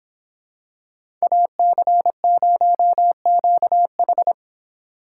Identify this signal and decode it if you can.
Morse code, heard as a keyed tone, about 26 wpm, 700 Hz. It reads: AC0QH